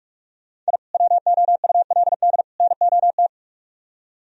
Morse code, heard as a keyed tone, about 60 wpm, spaced as for 52 wpm, 700 Hz. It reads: I WOULD NOT
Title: Speed-Racing - Sets of 3 Words 45wpm